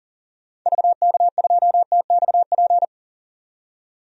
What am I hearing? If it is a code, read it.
Morse code, heard as a keyed tone, about 40 wpm, 700 Hz. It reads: VK2TXP
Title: DXCC-100 Call Signs - 6 Characters - 40wpm